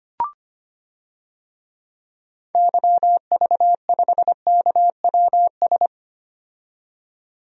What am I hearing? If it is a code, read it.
Morse code, heard as a keyed tone, about 25 wpm, 700 Hz. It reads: YV5KWS
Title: DXCC-100 Call Signs - 6 Characters - 25wpm